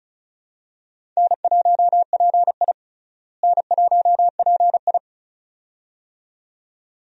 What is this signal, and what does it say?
Morse code, heard as a keyed tone, about 36 wpm, 700 Hz. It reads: N1PI N1PI
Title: USA Call Signs - 4 Characters - Repeated 2x - 35wpm